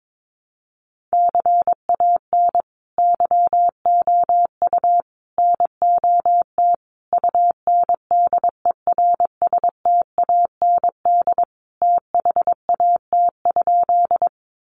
Morse code, heard as a keyed tone, about 22 wpm, 700 Hz. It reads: CAN YOU NOT UNDERSTAND THAT?
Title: Sentences from Top 600 Words 22wpm